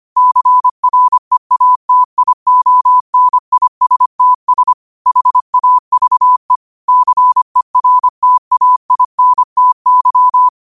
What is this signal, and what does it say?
Morse code, heard as a keyed tone, about 25 wpm, 1000 Hz. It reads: CREATIONISTS HAVE CERTAINTY
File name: qotdfiles / 25wpm